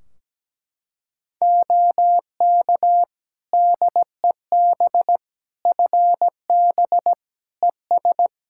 Morse code, heard as a keyed tone, about 17 wpm, 700 Hz. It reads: OK DEB FB ES
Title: Practice QSO - Exchange 2 - Info - 17wpm